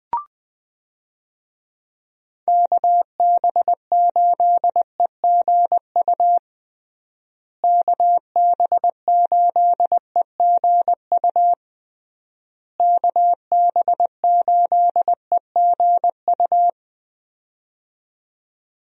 Morse code, heard as a keyed tone, about 21 wpm, 700 Hz. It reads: KB8EGU KB8EGU KB8EGU
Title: USA Call Signs - 6 Characters - Repeated 3x - 20wpm